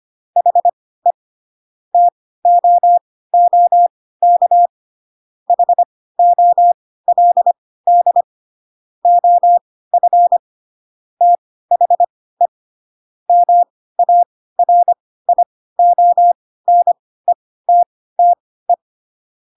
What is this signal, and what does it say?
Morse code, heard as a keyed tone, about 25 wpm, spaced as for 16 wpm, 700 Hz. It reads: HE TOOK HOLD OF THE MARIONETTE
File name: Pinocchio / Pinocchio ch3 10 25 ik1hge